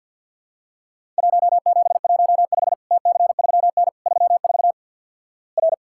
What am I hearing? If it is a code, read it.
Morse code, heard as a keyed tone, about 48 wpm, 700 Hz. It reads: 1715 TC3N 34 R
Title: Worked All Europe DX Contest - QTC Exchange with Repeats - 50wpm